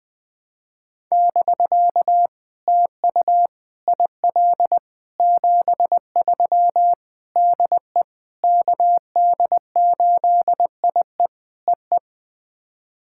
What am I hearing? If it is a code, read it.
Morse code, heard as a keyed tone, about 20 wpm, 700 Hz. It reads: <BK> TU IL 73 DE KD8IE EE
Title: POTA QSO Practice - 20wpm